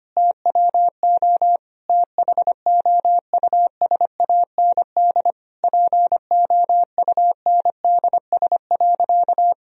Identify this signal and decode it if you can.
Morse code, heard as a keyed tone, about 25 wpm, 700 Hz. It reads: TWO THOUSAND POUNDS.
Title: Sentences from Top 1000 Words - Review Part 4 - 25wpm